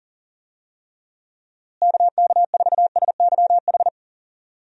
Morse code, heard as a keyed tone, about 37 wpm, 700 Hz. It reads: KK4SYH
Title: USA Call Signs - 6 Characters - Repeated 3x - 40wpm